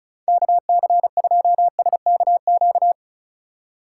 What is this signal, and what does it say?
Morse code, heard as a keyed tone, about 36 wpm, 700 Hz. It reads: KC2SKQ
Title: USA Call Signs - 6 Characters - 2x Character Spacing - 35wpm